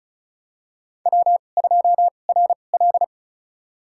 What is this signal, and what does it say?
Morse code, heard as a keyed tone, about 36 wpm, spaced as for 25 wpm, 700 Hz. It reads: W2RL
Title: USA Call Signs - 2x Character Spacing - 35wpm